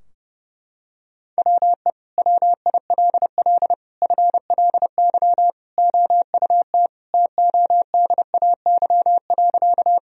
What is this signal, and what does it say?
Morse code, heard as a keyed tone, about 32 wpm, 700 Hz. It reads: WE WILL FLY OUT TODAY.
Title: Sentences from Top 1000 Words - Review Part 2 - 30wpm